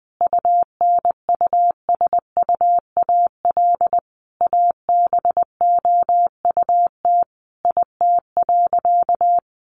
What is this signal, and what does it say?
Morse code, heard as a keyed tone, about 20 wpm, 700 Hz. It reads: UNUSUAL ABOUT IT.
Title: Sentences from Top 1400 Words 20wpm